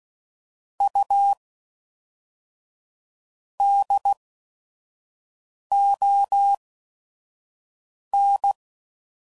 Morse code, heard as a keyed tone, about 16 wpm, spaced as for 5 wpm, 800 Hz. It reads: U DON
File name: CW Code Groups @5WPM / QSOs / QSO WD0FBT